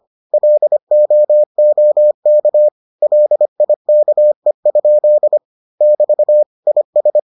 Morse code, heard as a keyed tone, about 25 wpm, 600 Hz. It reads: LOOK LIKE? <BT> IS